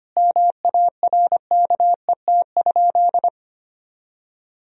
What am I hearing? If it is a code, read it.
Morse code, heard as a keyed tone, about 25 wpm, 700 Hz. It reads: MARKET?
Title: Sentences from Top 1000 Words - Review Part 3 - 25wpm